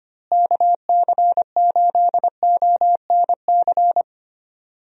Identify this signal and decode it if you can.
Morse code, heard as a keyed tone, about 25 wpm, 700 Hz. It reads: KC8ONC
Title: USA Call Signs - 6 Characters - Repeated 3x - 25wpm